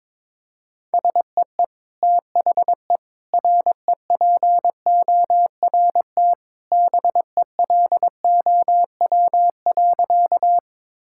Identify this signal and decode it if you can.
Morse code, heard as a keyed tone, about 22 wpm, 700 Hz. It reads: SEE THE REPORT BELOW.
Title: Sentences from Top 1000 Words - Review Part 6 - 22wpm